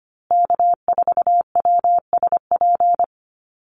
Morse code, heard as a keyed tone, about 25 wpm, 700 Hz. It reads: K4WSP
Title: USA Call Signs Repeated 2x 25wpm